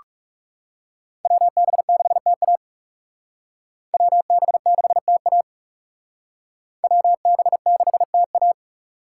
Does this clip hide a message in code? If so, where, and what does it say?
Yes: Morse code from 1.3 to 8.5 s, heard as a keyed tone, about 38 wpm, 700 Hz. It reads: WB6TA WB6TA WB6TA